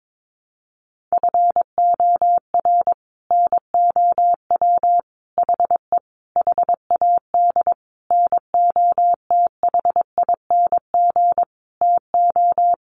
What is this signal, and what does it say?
Morse code, heard as a keyed tone, about 22 wpm, 700 Hz. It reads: FOR NOW HE HAD NOTHING TO
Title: Sentences from Top 1100 Words 22wpm